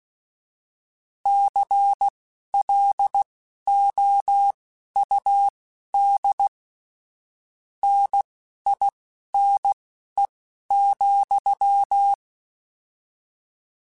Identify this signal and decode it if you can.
Morse code, heard as a keyed tone, about 16 wpm, spaced as for 11 wpm, 800 Hz. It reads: CLOUD NINE,
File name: Misc / Misc-mp3s / Word Pairs @10